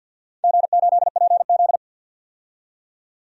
Morse code, heard as a keyed tone, about 53 wpm, 700 Hz. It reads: G8PZ
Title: DXCC-100 Call Signs - 4 Characters - 50wpm